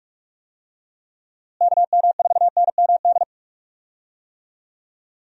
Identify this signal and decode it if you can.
Morse code, heard as a keyed tone, about 46 wpm, 700 Hz. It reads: KM4NMD
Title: USA Call Signs - 6 Characters - Repeated 2x - 45wpm